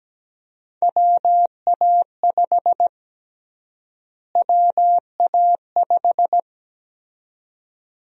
Morse code, heard as a keyed tone, about 17 wpm, 700 Hz. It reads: WA5 WA5
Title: USA Call Sign Prefixes - Repeated 2x - 17wpm